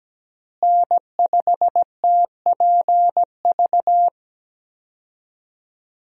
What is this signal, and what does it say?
Morse code, heard as a keyed tone, about 17 wpm, 700 Hz. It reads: N5TPV
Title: USA Call Signs - 5 Characters - Repeated 2x - 17wpm